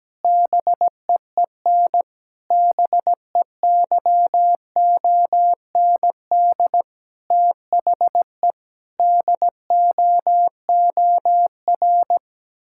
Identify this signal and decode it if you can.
Morse code, heard as a keyed tone, about 17 wpm, 700 Hz. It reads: BEEN BEYOND THE DOOR
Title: Sentences from Top 1000 Words - Review Part 6 - 17wpm